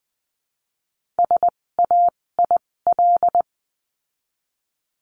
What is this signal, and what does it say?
Morse code, heard as a keyed tone, about 20 wpm, spaced as for 16 wpm, 700 Hz. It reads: SAIL
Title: 256 Lesson 42a - Words 12-20wpm